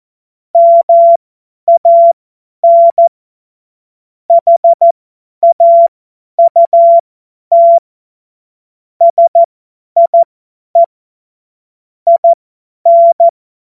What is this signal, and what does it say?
Morse code, heard as a keyed tone, about 14 wpm, spaced as for 10 wpm, 680 Hz. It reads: MAN HAUT SIE IN
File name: Witz-15-Juli-2025-50-F70